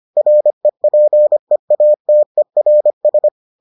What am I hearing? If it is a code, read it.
Morse code, heard as a keyed tone, about 25 wpm, 600 Hz. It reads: REPEATERS